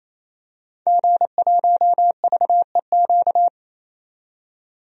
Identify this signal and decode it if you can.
Morse code, heard as a keyed tone, about 27 wpm, 700 Hz. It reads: G1VEQ
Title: DXCC-100 Call Signs - 5 Characters - 28wpm